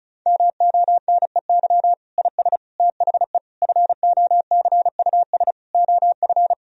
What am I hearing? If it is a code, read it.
Morse code, heard as a keyed tone, about 36 wpm, 700 Hz. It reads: MONEY IS THE FOCUS OF
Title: Sentences from Top 1000 Words - Review Part 4 - 35wpm